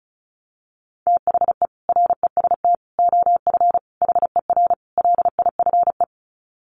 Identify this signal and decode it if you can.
Morse code, heard as a keyed tone, about 34 wpm, 700 Hz. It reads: THE REST OF HER LIFE
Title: Sets of 5 Words Encore 35wpm